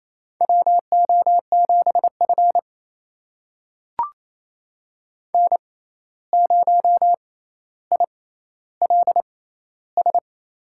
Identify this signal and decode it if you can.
Morse code, heard as a keyed tone, about 28 wpm, spaced as for 13 wpm, 700 Hz. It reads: WO7F N0ILS
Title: USA Call Signs - 6x Character Spacing - 28wpm